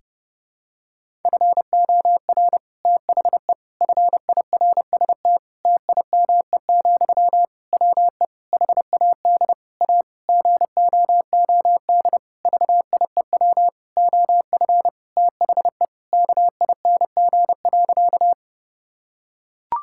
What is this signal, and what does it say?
Morse code, heard as a keyed tone, about 30 wpm, 700 Hz. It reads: FOR THE FIRST TIME, WE HAD A GOOD VIEW OF THE KING.
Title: Sentences from Top 1000 Words - Review Part 6 - 30wpm